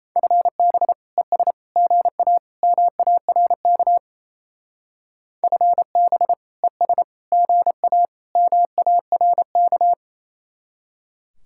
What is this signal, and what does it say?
Morse code, heard as a keyed tone, about 35 wpm, 700 Hz. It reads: FB ES GA MARK FB ES GA MARK
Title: Speed-Racing Common QSO Phrases Encore - 28wpm